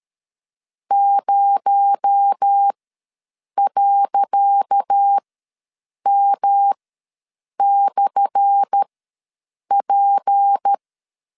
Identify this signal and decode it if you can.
Morse code, heard as a keyed tone, about 13 wpm, spaced as for 7 wpm, 800 Hz. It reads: 0.M/P
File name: LEZIONI / 10 SIGNS / 10C DOT ws 12wpm 3548